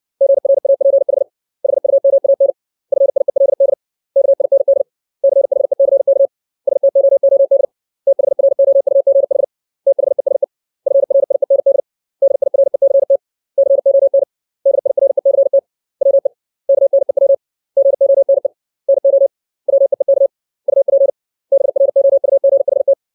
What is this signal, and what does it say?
Morse code, heard as a keyed tone, about 60 wpm, 550 Hz. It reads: GRAPH HUMAN FIELD KIND CHECK STOOD THROUGH THESE FRIEND DIRECT CON DIRECT WE KNEW GONE NO WEEK UP BROUGHT
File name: 632 words / 632 Words 60WPM